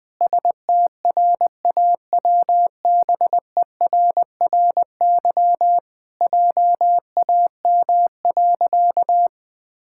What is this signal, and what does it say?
Morse code, heard as a keyed tone, about 20 wpm, 700 Hz. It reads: STRAWBERRY JAM.
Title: Pangram Challenge v2 - 20wpm